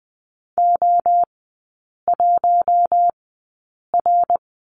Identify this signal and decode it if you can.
Morse code, heard as a keyed tone, about 20 wpm, spaced as for 8 wpm, 700 Hz. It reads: O1R
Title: 060 Lesson 11e - Letter h - Three Characters 20wpm